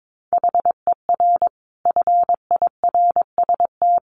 Morse code, heard as a keyed tone, about 22 wpm, 700 Hz. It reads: HER FIRST
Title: Additional Easy Phrases 22wpm